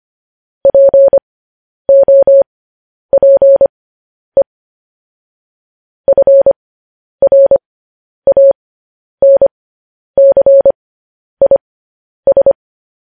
Morse code, heard as a keyed tone, about 25 wpm, spaced as for 10 wpm, 550 Hz. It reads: POPE FRANCIS
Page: news headlines in morse code at 10 wpm